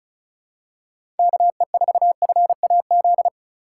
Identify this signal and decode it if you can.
Morse code, heard as a keyed tone, about 36 wpm, 700 Hz. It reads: KE4FAZ
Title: USA Call Signs - 6 Characters - 2x Character Spacing - 35wpm